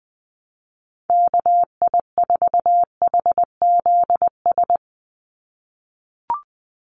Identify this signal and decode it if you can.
Morse code, heard as a keyed tone, about 19 wpm, 700 Hz. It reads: KI4HZS
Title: USA Call Signs 20wpm